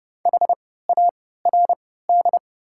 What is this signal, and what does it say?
Morse code, heard as a keyed tone, about 32 wpm, spaced as for 17 wpm, 700 Hz. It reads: HARD
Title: 3x Character Spacing - Top 500 Words - 30wpm